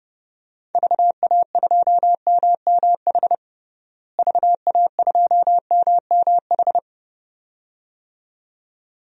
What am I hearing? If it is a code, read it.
Morse code, heard as a keyed tone, about 29 wpm, 700 Hz. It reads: VA2MMH VA2MMH
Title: DXCC-100 Call Signs - 6 Characters - Repeated 2x - 30wpm